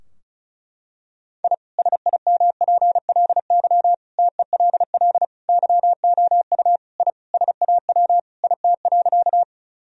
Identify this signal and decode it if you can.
Morse code, heard as a keyed tone, about 34 wpm, 700 Hz. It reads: I SIMPLY TELL YOU I SAW IT.
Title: Sentences from Top 1000 Words - Review Part 4 - 35wpm